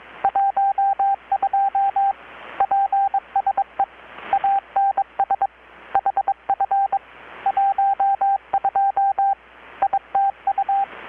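Morse code, heard as a keyed tone, about 22 wpm, 790 Hz, with radio noise. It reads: 12 PSE ANS HF 12 ITU
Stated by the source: Morse from KPH's Night of Nights XX maritime radiotelegraphy event, received in Chiapas, Mexico about 3,800 km away on a Kenwood R-600 with a grounded 10 m long-wire antenna.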